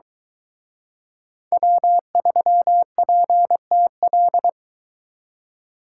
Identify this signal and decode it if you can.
Morse code, heard as a keyed tone, about 23 wpm, 700 Hz. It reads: W3PTL